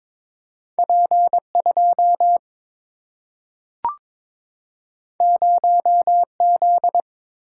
Morse code, heard as a keyed tone, about 22 wpm, 700 Hz. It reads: P 2 0 Z
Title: Single-Letter Single-Number 22wpm